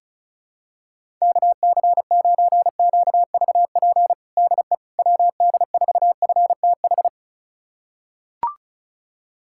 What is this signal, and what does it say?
Morse code, heard as a keyed tone, about 36 wpm, 700 Hz. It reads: KC9QVP DE WD4FTH
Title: USA Callsign DE Callsign - 6 Characters - 35wpm